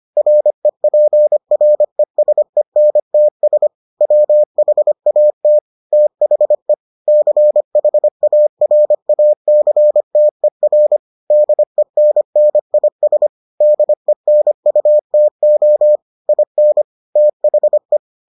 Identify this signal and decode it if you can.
Morse code, heard as a keyed tone, about 25 wpm, 600 Hz. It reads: REPRESENTS WHAT THE CHARACTER DENNIS DENUTO IN THE